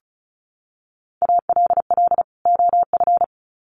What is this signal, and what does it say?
Morse code, heard as a keyed tone, about 34 wpm, 700 Hz. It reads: ALL OF
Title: Sets of 2 Words 35wpm